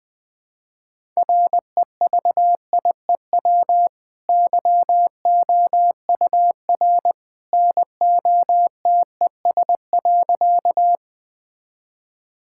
Morse code, heard as a keyed tone, about 20 wpm, 700 Hz. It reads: REVIEW YOUR NOTES.
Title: Easy Sentences with Hints - 20wpm